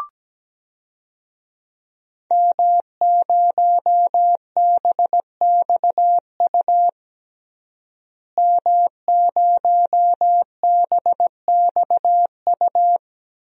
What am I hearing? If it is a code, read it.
Morse code, heard as a keyed tone, about 17 wpm, 700 Hz. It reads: M0BXU M0BXU
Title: DXCC-100 Call Signs - 5 Characters - Repeated 3x - 17wpm